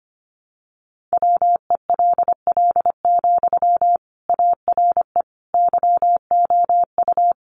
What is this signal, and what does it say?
Morse code, heard as a keyed tone, about 25 wpm, 700 Hz. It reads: WELL, ARE YOU
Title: Sentences from Top 800 Words 25wpm